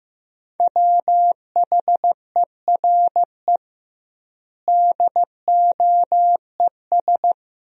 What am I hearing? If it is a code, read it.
Morse code, heard as a keyed tone, about 15 wpm, 700 Hz. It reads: WHERE DOES
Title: 2x Word-Spacing - Sentences from Top 200 Words - 15wpm